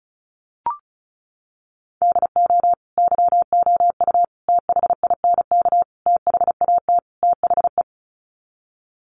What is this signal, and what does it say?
Morse code, heard as a keyed tone, about 34 wpm, 700 Hz. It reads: DO YOU THINK THAT THE
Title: Sets of 5 Words Encore 35wpm